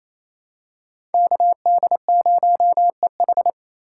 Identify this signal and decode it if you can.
Morse code, heard as a keyed tone, about 28 wpm, 700 Hz. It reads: KD0EH
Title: USA Call Signs - 5 Characters - 2x Character Spacing - 28wpm